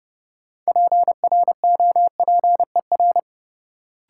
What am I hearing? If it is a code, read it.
Morse code, heard as a keyed tone, about 30 wpm, 700 Hz. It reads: PROPER
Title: Sentences from Top 800 Words - New Word Drill - 30wpm